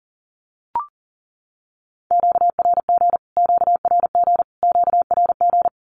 Morse code, heard as a keyed tone, about 40 wpm, 700 Hz. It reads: QRG QRG QRG
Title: Q-Codes Repeated 3x 40wpm